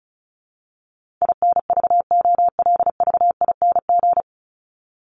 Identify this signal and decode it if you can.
Morse code, heard as a keyed tone, about 34 wpm, 700 Hz. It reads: INVOLVING
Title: ING Suffix - Top 100 Words with Base Words 35wpm